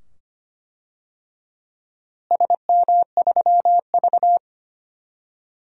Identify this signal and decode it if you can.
Morse code, heard as a keyed tone, about 25 wpm, 700 Hz. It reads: SM3V
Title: DXCC-100 Call Signs - 4 Characters - Repeated 2x - 25wpm